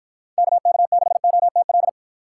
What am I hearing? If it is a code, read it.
Morse code, heard as a keyed tone, about 53 wpm, 700 Hz. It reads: KK6OTL